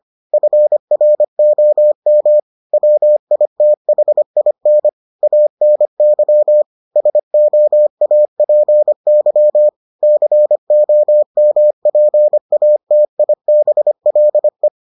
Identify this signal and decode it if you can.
Morse code, heard as a keyed tone, about 25 wpm, 600 Hz. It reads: FROM WITHIN ANY SOAPY COMPATIBLE